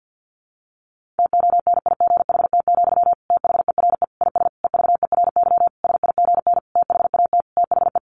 Morse code, heard as a keyed tone, about 50 wpm, 700 Hz. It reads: TONIGHT, THERE IS EVERY SIGN THAT THE
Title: Sentences from Top 1400 Words 50wpm